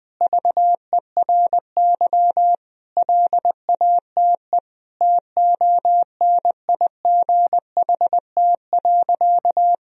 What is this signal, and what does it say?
Morse code, heard as a keyed tone, about 20 wpm, 700 Hz. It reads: VERY LATE TONIGHT.
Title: Sentences from Top 1000 Words - Review Part 4 - 20wpm